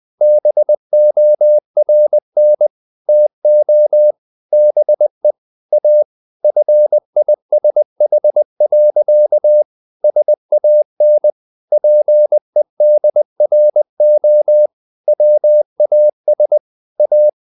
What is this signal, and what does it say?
Morse code, heard as a keyed tone, about 20 wpm, 600 Hz. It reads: BORN TO BE A FISH. SAN PEDRO WAS A
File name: the old man and the sea part 2 20wpm